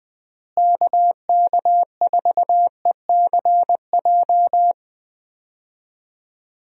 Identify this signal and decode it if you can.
Morse code, heard as a keyed tone, about 20 wpm, 700 Hz. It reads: KK4ECJ
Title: USA Call Signs - 6 Characters - Repeated 3x - 20wpm